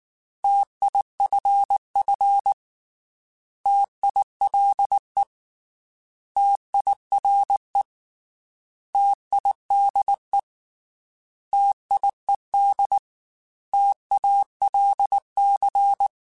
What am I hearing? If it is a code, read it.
Morse code, heard as a keyed tone, about 19 wpm, 800 Hz. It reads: TIFF TILE TIRE TIDE TIED TALC
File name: T Prefixes / T mp3s